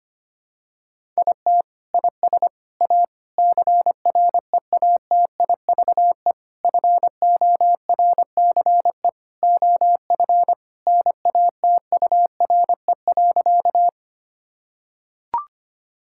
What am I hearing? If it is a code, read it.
Morse code, heard as a keyed tone, about 25 wpm, 700 Hz. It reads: IT IS A CREATIVE FORCE OF NATURE.
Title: Sentences from Top 1000 Words - Review Part 5 - 25wpm